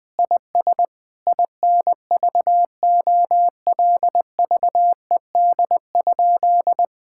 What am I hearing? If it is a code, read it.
Morse code, heard as a keyed tone, about 20 wpm, 700 Hz. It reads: IS INVOLVED?
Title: Sentences from Top 1000 Words - Review Part 5 - 20wpm